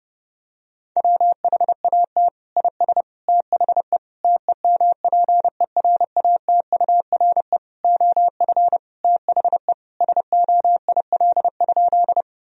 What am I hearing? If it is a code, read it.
Morse code, heard as a keyed tone, about 30 wpm, 700 Hz. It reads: WHAT IS THE TEMPERATURE OF THE SOIL?
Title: Sentences from Top 1000 Words - Review Part 4 - 30wpm